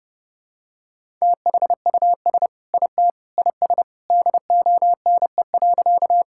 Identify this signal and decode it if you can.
Morse code, heard as a keyed tone, about 30 wpm, 700 Hz. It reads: THUS IT IS DONE.
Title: Sentences from Top 1000 Words - Review Part 5 - 30wpm